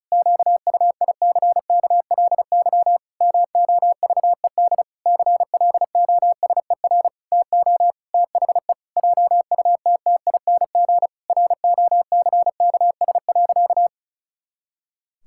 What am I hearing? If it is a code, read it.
Morse code, heard as a keyed tone, about 34 wpm, 700 Hz. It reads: QUICKLY MOVED CLOSER TO THE JUTTING ROCKS.
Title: Pangram Challenge v2 - 35wpm